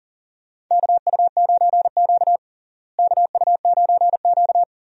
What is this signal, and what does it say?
Morse code, heard as a keyed tone, about 37 wpm, 700 Hz. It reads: KU9Q KU9Q
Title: USA Call Signs - 4 Characters - Repeated 2x - 40wpm